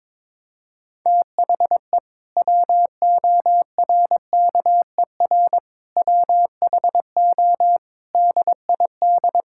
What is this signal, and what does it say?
Morse code, heard as a keyed tone, about 22 wpm, 700 Hz. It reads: THE WORKER WHO DID
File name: Sentences with 7-Word Phrases - 22wpm